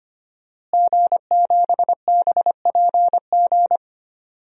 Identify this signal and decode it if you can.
Morse code, heard as a keyed tone, about 25 wpm, 700 Hz. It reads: G7BPG
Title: DXCC-100 Call Signs - 5 Characters - Repeated 2x - 25wpm